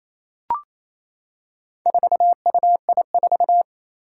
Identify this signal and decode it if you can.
Morse code, heard as a keyed tone, about 28 wpm, 700 Hz. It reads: 4UI4
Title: Four Challenging Letter-Number Combinations 28wpm